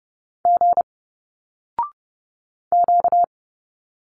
Morse code, heard as a keyed tone, about 30 wpm, 700 Hz. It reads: G Q
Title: Single Letters 30wpm